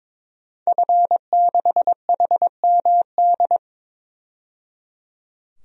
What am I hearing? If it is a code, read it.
Morse code, heard as a keyed tone, about 22 wpm, 700 Hz. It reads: F6HMD
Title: DXCC-100 Call Signs - 5 Characters - 22wpm